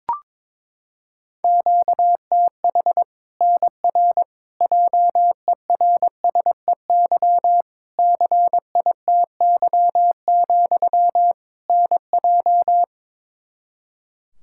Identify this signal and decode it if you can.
Morse code, heard as a keyed tone, about 22 wpm, 700 Hz. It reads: QTH NR JERSEY CITY, NJ
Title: K1USN SST QSO Practice 22wpm